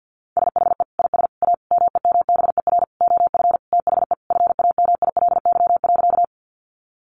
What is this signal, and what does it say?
Morse code, heard as a keyed tone, about 50 wpm, 700 Hz. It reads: SHE IS A MEMBER OF THE FAMILY.
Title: Sentences from Top 1100 Words 50wpm